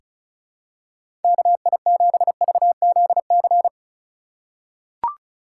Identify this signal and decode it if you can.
Morse code, heard as a keyed tone, about 34 wpm, 700 Hz. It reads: KI7VZC
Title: USA Call Signs - 6 Characters - 2x Character Spacing - 35wpm